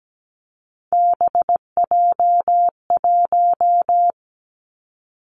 Morse code, heard as a keyed tone, about 17 wpm, 700 Hz. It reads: BJ1
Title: Three Challenging Letter-Number Combinations 17wpm